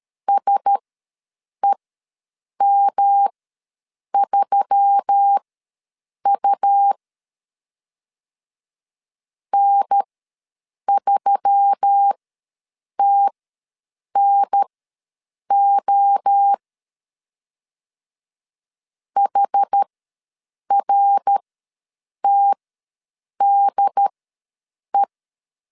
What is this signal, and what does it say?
Morse code, heard as a keyed tone, about 13 wpm, spaced as for 7 wpm, 800 Hz. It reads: SEM3U N3TNO HRTDE
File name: LEZIONI / 5 LBVC / 5 L ws 12wpm 2958